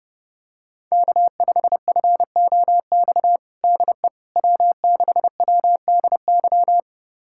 Morse code, heard as a keyed tone, about 30 wpm, 700 Hz. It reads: K5FOX DE W6WDY
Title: USA Callsign DE Callsign - 5 Characters - 30wpm